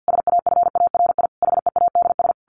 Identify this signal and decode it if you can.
Morse code, heard as a keyed tone, about 50 wpm, 700 Hz. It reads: SAFARI HEADS
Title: A Sound Of Thunder 50wpm